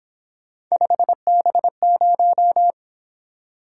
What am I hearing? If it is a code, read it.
Morse code, heard as a keyed tone, about 26 wpm, 700 Hz. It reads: 5B0
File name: Speed-Racing - Three Letter-Number Combinations - 22wpm